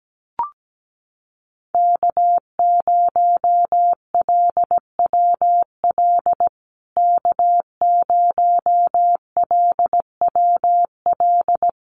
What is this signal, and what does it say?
Morse code, heard as a keyed tone, about 17 wpm, 700 Hz. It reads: K0LWL K0LWL
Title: USA Call Signs Repeated 3x 17wpm